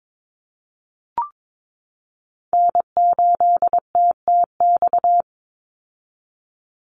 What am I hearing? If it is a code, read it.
Morse code, heard as a keyed tone, about 22 wpm, 700 Hz. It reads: N8TTX
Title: USA Call Signs 22wpm